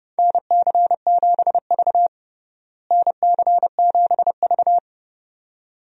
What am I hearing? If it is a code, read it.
Morse code, heard as a keyed tone, about 30 wpm, 700 Hz. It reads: NC7V NC7V
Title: USA Call Signs - 4 Characters - Repeated 2x - 30wpm